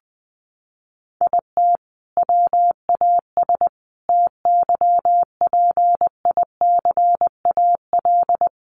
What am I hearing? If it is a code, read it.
Morse code, heard as a keyed tone, about 20 wpm, 700 Hz. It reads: IT WAS TYPICAL
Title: Sentences from Top 1100 Words 20wpm